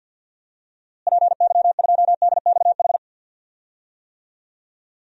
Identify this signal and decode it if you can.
Morse code, heard as a keyed tone, about 52 wpm, 700 Hz. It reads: PY2DXH
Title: DXCC-100 Call Signs - 6 Characters - Repeated 3x - 50wpm